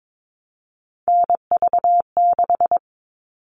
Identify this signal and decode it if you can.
Morse code, heard as a keyed tone, about 22 wpm, 700 Hz. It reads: NV6
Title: All Cut Numbers - 3-Digit 22wpm